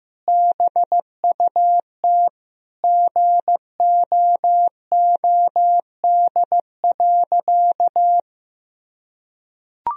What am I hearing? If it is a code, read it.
Morse code, heard as a keyed tone, about 15 wpm, 700 Hz. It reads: BUT GOOD.
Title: Sentences from Top 1000 Words - Review Part 2 - 15wpm